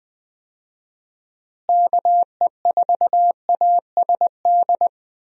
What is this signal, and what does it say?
Morse code, heard as a keyed tone, about 19 wpm, 700 Hz. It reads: KE4ASD
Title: USA Call Signs - 6 Characters - Repeated 2x - 20wpm